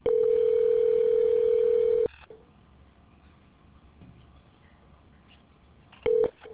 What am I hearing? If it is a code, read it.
Ringback tone (North America), 440 + 480 Hz.